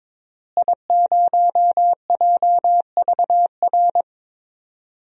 Morse code, heard as a keyed tone, about 22 wpm, 700 Hz. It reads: I0JVR
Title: DXCC-100 Call Signs - 5 Characters - Repeated 3x - 22wpm